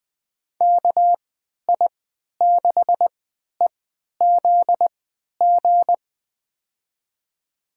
Morse code, heard as a keyed tone, about 20 wpm, spaced as for 11 wpm, 700 Hz. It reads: KI6EZG
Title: USA Call Signs - 6 Characters - 3x Character Spacing - 20wpm